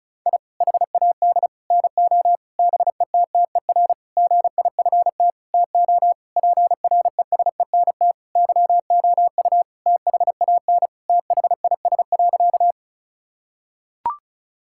Morse code, heard as a keyed tone, about 36 wpm, 700 Hz. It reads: I HAD NO BETTER GIFT TO PRESENT YOU THAN THIS.
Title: Sentences from Top 1000 Words - Review Part 6 - 35wpm